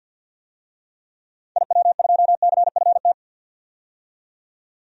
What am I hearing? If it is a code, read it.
Morse code, heard as a keyed tone, about 47 wpm, 700 Hz. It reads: IW2CFT
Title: DXCC-100 Call Signs - 6 Characters - 50wpm